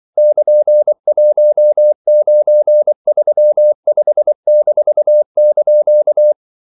Morse code, heard as a keyed tone, about 24 wpm, 600 Hz. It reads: <KN>1935-)
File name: CW / Practice / EnglishQuotations0324